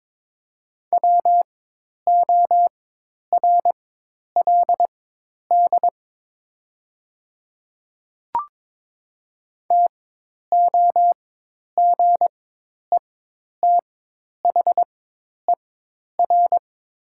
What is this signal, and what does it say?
Morse code, heard as a keyed tone, about 22 wpm, spaced as for 10 wpm, 700 Hz. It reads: WORLD TOGETHER
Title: 4x Character Spacing - Top 500 Words - 22wpm